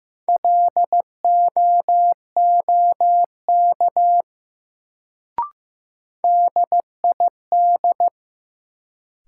Morse code, heard as a keyed tone, about 15 wpm, 700 Hz. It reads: LOOK DID
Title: Sentences from Top 100 Words - New Word Drill - 15wpm